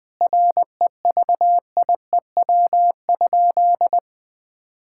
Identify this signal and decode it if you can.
Morse code, heard as a keyed tone, about 20 wpm, 700 Hz. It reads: REVIEW?
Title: Sentences from Top 1000 Words - Review Part 6 - 20wpm